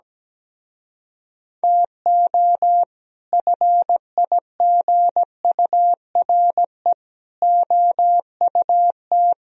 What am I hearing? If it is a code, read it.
Morse code, heard as a keyed tone, about 17 wpm, 700 Hz. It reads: TO FIGURE OUT
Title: Speed-Racing - Sets of 5 Words 17wpm